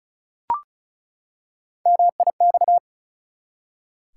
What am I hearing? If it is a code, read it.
Morse code, heard as a keyed tone, about 36 wpm, 700 Hz. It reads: MIX
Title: Three Letter Words - 35wpm